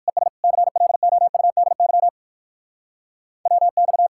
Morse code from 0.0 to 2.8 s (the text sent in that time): ES CLOUDY